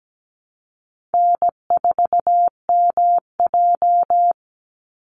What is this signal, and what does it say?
Morse code, heard as a keyed tone, about 17 wpm, 700 Hz. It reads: N4MJ
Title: NA SOTA Chaser Callsigns 17wpm